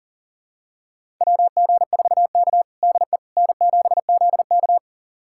Morse code, heard as a keyed tone, about 40 wpm, 700 Hz. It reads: WG4K DE N7ZK
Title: USA Callsign DE Callsign - 4 Characters - 40wpm